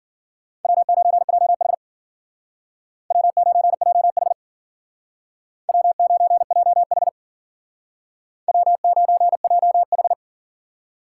Morse code, heard as a keyed tone, about 48 wpm, 700 Hz. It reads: W9JH W9JH W9JH W9JH
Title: Speed Racing - USA 1x2 Call Signs 40wpm